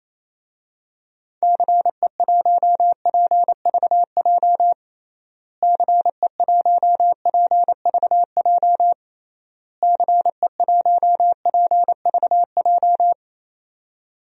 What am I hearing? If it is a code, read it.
Morse code, heard as a keyed tone, about 28 wpm, 700 Hz. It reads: CE1PVJ CE1PVJ CE1PVJ
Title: DXCC-100 Call Signs - 6 Characters - Repeated 3x - 28wpm